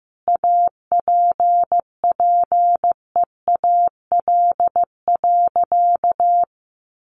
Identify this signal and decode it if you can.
Morse code, heard as a keyed tone, about 15 wpm, 700 Hz. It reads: APPEAL.
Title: Sentences from Top 1300 Words 15wpm